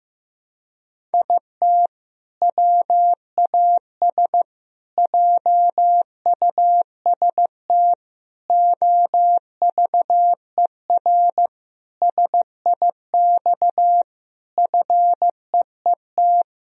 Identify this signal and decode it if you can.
Morse code, heard as a keyed tone, about 15 wpm, 700 Hz. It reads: IT WAS JUST OVER SIX FEET
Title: Sentences from Top 1000 Words - Review Part 2 - 15wpm